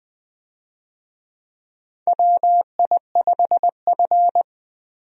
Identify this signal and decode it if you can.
Morse code, heard as a keyed tone, about 20 wpm, 700 Hz. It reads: WI5F
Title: USA Call Signs - 4 Characters - Repeated 3x - 20wpm